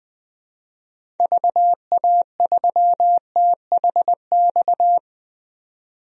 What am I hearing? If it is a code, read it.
Morse code, heard as a keyed tone, about 21 wpm, 700 Hz. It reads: VA3THX
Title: DXCC-100 Call Signs - 6 Characters - 20wpm